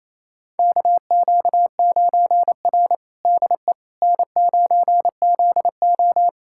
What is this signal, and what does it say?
Morse code, heard as a keyed tone, about 28 wpm, 700 Hz. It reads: KQ9R DE N9ZO
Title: USA Callsign DE Callsign - 4 Characters - 28wpm